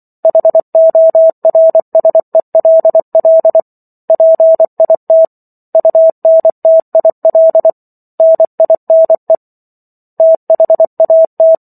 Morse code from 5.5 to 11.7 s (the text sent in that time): UNTIL NINE THAT